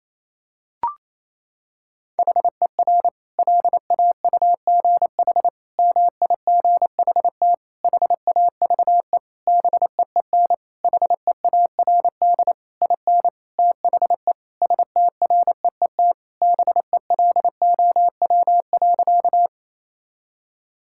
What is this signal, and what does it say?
Morse code, heard as a keyed tone, about 27 wpm, 700 Hz. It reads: HER LAUGH MIGHT HAVE BEEN HEARD IN THE STREET BELOW.
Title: Sentences from Top 1000 Words - Review Part 5 - 28wpm